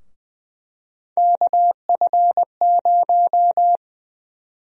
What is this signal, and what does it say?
Morse code, heard as a keyed tone, about 20 wpm, 700 Hz. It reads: KF0
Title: USA Call Sign Prefixes - Repeated 2x - 20wpm